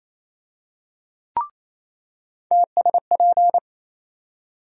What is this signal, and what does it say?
Morse code, heard as a keyed tone, about 28 wpm, 700 Hz. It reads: TSP
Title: USA Call Sign Postfixes - 3 Letters 28wpm